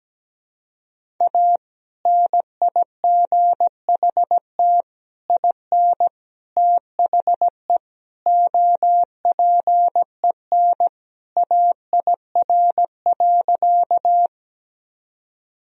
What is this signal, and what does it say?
Morse code, heard as a keyed tone, about 17 wpm, 700 Hz. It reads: A NIGHT IN THE OPEN AIR.
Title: Sentences from Top 1000 Words - Review Part 1 - 17wpm